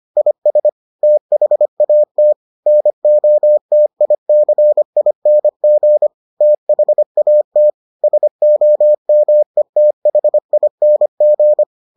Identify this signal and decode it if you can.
Morse code, heard as a keyed tone, about 25 wpm, 600 Hz. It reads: IS THAT NOTICING THAT SOMETHING